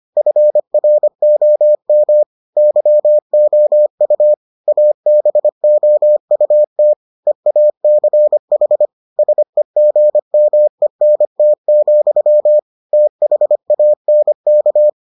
Morse code, heard as a keyed tone, about 25 wpm, 600 Hz. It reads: FROM YOU ABOUT EACH SEGMENT, THANK